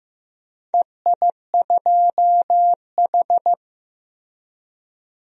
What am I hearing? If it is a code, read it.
Morse code, heard as a keyed tone, about 16 wpm, 700 Hz. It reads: EI2H
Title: DXCC-100 Call Signs - 4 Characters - Repeated 2x - 15wpm